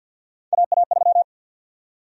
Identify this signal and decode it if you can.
Morse code, heard as a keyed tone, about 52 wpm, 700 Hz. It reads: AA3
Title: USA Call Sign Prefixes - 50wpm